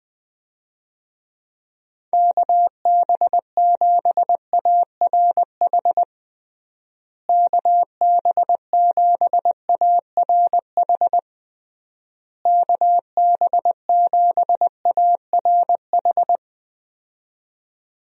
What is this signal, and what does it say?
Morse code, heard as a keyed tone, about 20 wpm, 700 Hz. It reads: KB7ARH KB7ARH KB7ARH
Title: USA Call Signs - 6 Characters - Repeated 3x - 20wpm